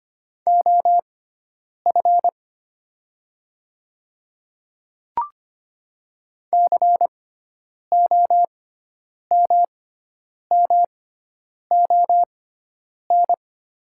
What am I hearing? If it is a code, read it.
Morse code, heard as a keyed tone, about 25 wpm, spaced as for 9 wpm, 700 Hz. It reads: OF COMMON
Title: 6x Character Spacing - Top 500 Words - 25wpm